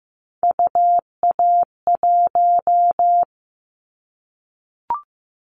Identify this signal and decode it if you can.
Morse code, heard as a keyed tone, about 15 wpm, 700 Hz. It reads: UA1
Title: Callsign Prefixes for Top 50 DXCC - 15wpm